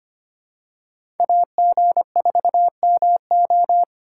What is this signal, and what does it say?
Morse code, heard as a keyed tone, about 25 wpm, 700 Hz. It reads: AG4MO
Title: USA Call Signs - 5 Characters - 2x Character Spacing - 25wpm